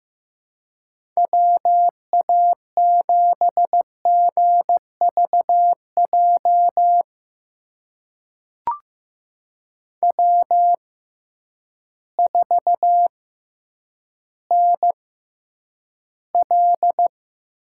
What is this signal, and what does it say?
Morse code, heard as a keyed tone, about 15 wpm, spaced as for 9 wpm, 700 Hz. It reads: WA7GVJ W4NL
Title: USA Call Signs - 6x Character Spacing - 15wpm